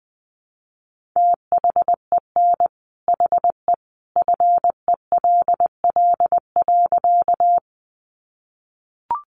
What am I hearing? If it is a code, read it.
Morse code, heard as a keyed tone, about 20 wpm, 700 Hz. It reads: THEN HE FELL.
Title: A Sound Of Thunder 20wpm